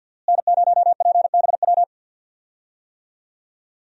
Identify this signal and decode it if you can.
Morse code, heard as a keyed tone, about 47 wpm, 700 Hz. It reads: N0PBW
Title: USA Call Signs - 5 Characters - Repeated 2x - 50wpm